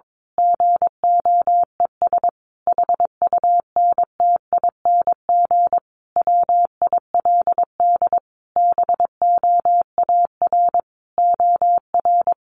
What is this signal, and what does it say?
Morse code, heard as a keyed tone, about 22 wpm, 700 Hz. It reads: GOES HUNTING WILD BOAR OR
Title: A Sound Of Thunder 22wpm